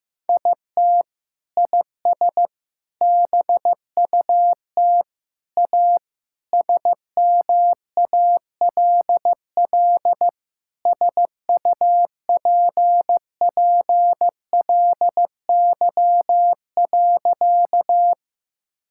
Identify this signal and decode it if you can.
Morse code, heard as a keyed tone, about 16 wpm, 700 Hz. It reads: IT IS BUT A SMALL SUPPLY.
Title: Sentences from Top 1000 Words - Review Part 5 - 15wpm